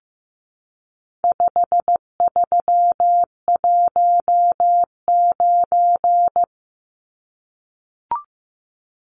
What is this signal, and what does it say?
Morse code, heard as a keyed tone, about 15 wpm, 700 Hz. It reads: 5319
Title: Numbers - 4-Digit 15wpm